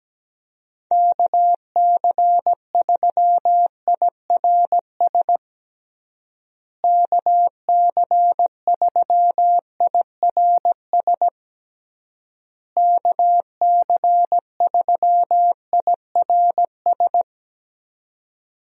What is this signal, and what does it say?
Morse code, heard as a keyed tone, about 17 wpm, 700 Hz. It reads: KC3IRS KC3IRS KC3IRS
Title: USA Call Signs - 6 Characters - Repeated 3x - 17wpm